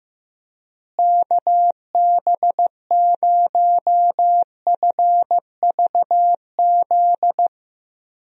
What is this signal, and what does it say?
Morse code, heard as a keyed tone, about 15 wpm, 700 Hz. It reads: KB0FVZ
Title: USA Call Signs - 6 Characters - Repeated 3x - 15wpm